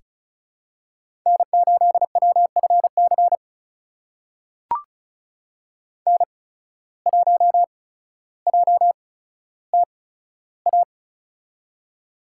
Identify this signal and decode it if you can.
Morse code, heard as a keyed tone, about 36 wpm, spaced as for 15 wpm, 700 Hz. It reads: N8WFC N1JTA
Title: USA Call Signs - 5 Characters - 8x Character Spacing - 35wpm